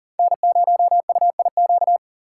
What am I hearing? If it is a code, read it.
Morse code, heard as a keyed tone, about 40 wpm, 700 Hz. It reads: N0UIQ
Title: USA Call Signs - 5 Characters - 6x Character Spacing - 40wpm